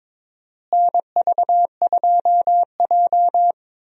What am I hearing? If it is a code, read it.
Morse code, heard as a keyed tone, about 21 wpm, 700 Hz. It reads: NV2J
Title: USA Call Signs - 4 Characters - Repeated 3x - 22wpm